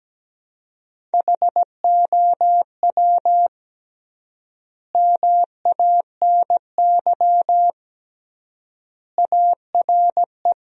Morse code, heard as a keyed tone, about 17 wpm, 700 Hz. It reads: HOW MANY ARE
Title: 3x Word-Spacing - Sentences from Top 300 Words - 17wpm